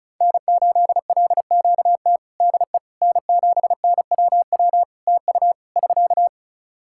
Morse code, heard as a keyed tone, about 36 wpm, 700 Hz. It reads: N8LQT DE N7NWW TU <SK>
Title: Practice QSO - Exchange 4 - Ending - 35wpm